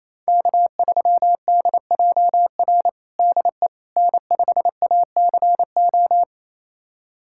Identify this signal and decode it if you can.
Morse code, heard as a keyed tone, about 28 wpm, 700 Hz. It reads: K3DJR DE N5ACO
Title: USA Callsign DE Callsign - 5 Characters - 28wpm